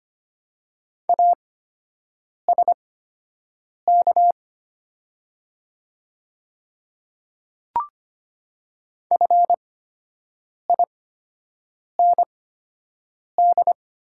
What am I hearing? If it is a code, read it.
Morse code, heard as a keyed tone, about 25 wpm, spaced as for 7 wpm, 700 Hz. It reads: ASK FIND
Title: 8x Character Spacing - Top 500 Words - 25wpm